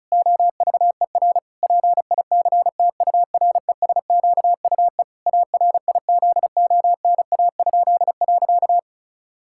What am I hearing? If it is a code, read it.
Morse code, heard as a keyed tone, about 34 wpm, 700 Hz. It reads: OVER PICTURESQUE ARIZONA?.
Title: Pangram Challenge v2 - 35wpm